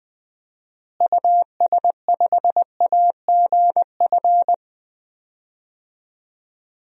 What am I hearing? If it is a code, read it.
Morse code, heard as a keyed tone, about 20 wpm, 700 Hz. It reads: US5AGF
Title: DXCC-100 Call Signs - 6 Characters - Repeated 2x - 20wpm